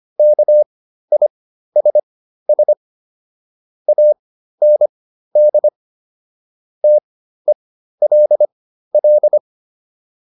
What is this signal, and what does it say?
Morse code, heard as a keyed tone, about 25 wpm, spaced as for 13 wpm, 600 Hz. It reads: KISS AND TELL